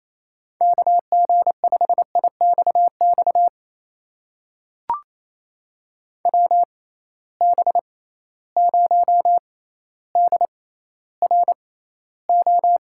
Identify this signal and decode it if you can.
Morse code, heard as a keyed tone, about 28 wpm, spaced as for 14 wpm, 700 Hz. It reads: KG5IXX WB0DRO
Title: USA Call Signs - 6 Characters - 6x Character Spacing - 28wpm